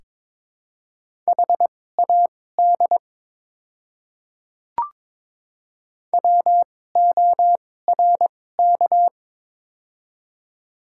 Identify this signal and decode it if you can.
Morse code, heard as a keyed tone, about 22 wpm, spaced as for 16 wpm, 700 Hz. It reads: HAD WORK
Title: 2x Character Spacing - Top 200 Words - 22wpm